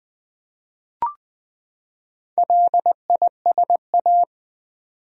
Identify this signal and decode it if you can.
Morse code, heard as a keyed tone, about 21 wpm, 700 Hz. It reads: LISA
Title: USA Common Names - Max 4 Letters - 20wpm